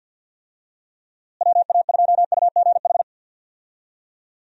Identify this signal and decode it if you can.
Morse code, heard as a keyed tone, about 46 wpm, 700 Hz. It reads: WA2UGH
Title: USA Call Signs - 6 Characters - Repeated 2x - 50wpm